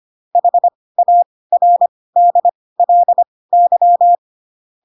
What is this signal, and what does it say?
Morse code, heard as a keyed tone, about 25 wpm, spaced as for 18 wpm, 700 Hz. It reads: HARDLY